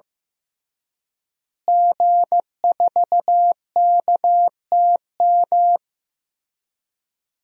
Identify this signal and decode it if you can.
Morse code, heard as a keyed tone, about 15 wpm, 700 Hz. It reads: G4KTM
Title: DXCC-100 Call Signs - 5 Characters - Repeated 3x - 15wpm